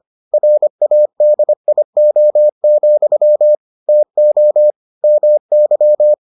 Morse code, heard as a keyed tone, about 25 wpm, 600 Hz. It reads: RADIO, TO MY